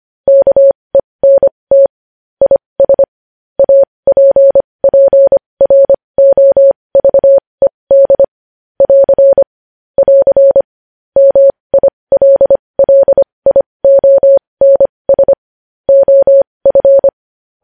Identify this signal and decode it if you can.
Morse code, heard as a keyed tone, about 25 wpm, spaced as for 20 wpm, 550 Hz. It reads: KENT IS APPROVED <AR> <AR> MILLIONS OF
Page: news-headlines-in-morse-code-at-20-wpm